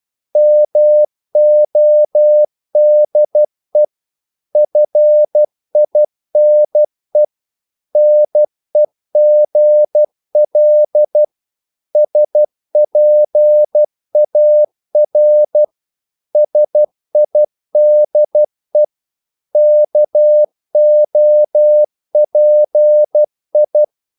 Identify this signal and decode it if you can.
Morse code, heard as a keyed tone, about 12 wpm, 600 Hz. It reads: MODE FINE NEGL SPAR SIDE KOPI
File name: Korte ord DK 12wpm